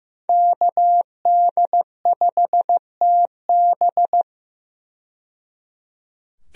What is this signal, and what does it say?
Morse code, heard as a keyed tone, about 15 wpm, 700 Hz. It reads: KD5TB
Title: USA Call Signs - 5 Characters - 15wpm